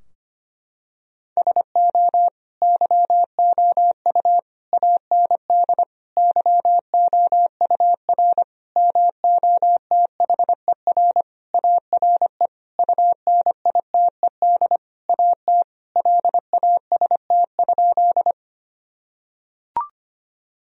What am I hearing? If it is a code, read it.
Morse code, heard as a keyed tone, about 25 wpm, 700 Hz. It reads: SO YOU AND YOUR MOTHER ARE UNITED AT LAST?
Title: Sentences from Top 1000 Words - Review Part 4 - 25wpm